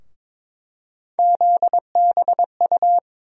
Morse code, heard as a keyed tone, about 21 wpm, 700 Hz. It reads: ZBU
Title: USA Call Sign Postfixes - 3 Letters 22wpm